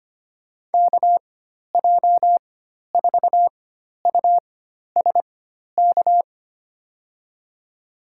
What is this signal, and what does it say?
Morse code, heard as a keyed tone, about 25 wpm, spaced as for 12 wpm, 700 Hz. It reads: KJ4USK
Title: USA Call Signs - 6 Characters - 4x Character Spacing - 25wpm